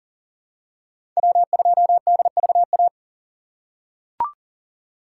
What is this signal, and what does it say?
Morse code, heard as a keyed tone, about 40 wpm, 700 Hz. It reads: W2DVA
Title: USA Call Signs - 5 Characters - 4x Character Spacing - 40wpm